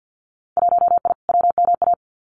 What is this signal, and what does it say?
Morse code, heard as a keyed tone, about 50 wpm, 700 Hz. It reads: 1I WMA
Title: Winter Field Day Exchanges Encore 50wpm